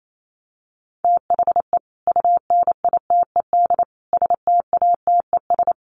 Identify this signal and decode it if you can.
Morse code, heard as a keyed tone, about 28 wpm, 700 Hz. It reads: THE UNITED STATES
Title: Sets of 4 Words 28wpm